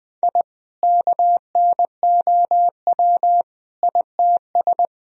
Morse code, heard as a keyed tone, about 20 wpm, 700 Hz. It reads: I KNOW ITS
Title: Sentences from Top 1000 Words - Review Part 2 - 20wpm